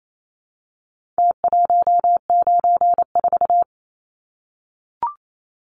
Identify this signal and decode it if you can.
Morse code, heard as a keyed tone, about 28 wpm, 700 Hz. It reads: T194
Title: Numbers - Contest Style - 4-Digit 28wpm